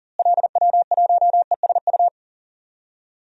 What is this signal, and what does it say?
Morse code, heard as a keyed tone, about 40 wpm, 700 Hz. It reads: LW1ESU
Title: DXCC-100 Call Signs - 6 Characters - Repeated 3x - 40wpm